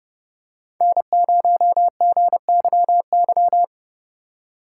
Morse code, heard as a keyed tone, about 30 wpm, 700 Hz. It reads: N0GYY